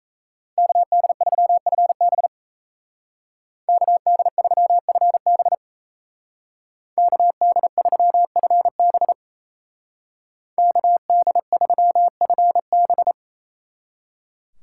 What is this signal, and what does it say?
Morse code, heard as a keyed tone, about 33 wpm, 700 Hz. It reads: KD3FB KD3FB KD3FB KD3FB
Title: Speed Racing - USA 2x2 Call Signs 28wpm